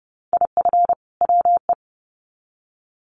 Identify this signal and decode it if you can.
Morse code, heard as a keyed tone, about 30 wpm, 700 Hz. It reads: IF WE
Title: Sets of 2 Words 30wpm